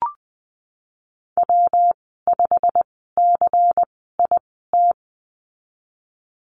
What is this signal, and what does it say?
Morse code, heard as a keyed tone, about 19 wpm, spaced as for 14 wpm, 700 Hz. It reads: W5CIT
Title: 128 Lesson 21g - Number 6 - Call Signs 20wpm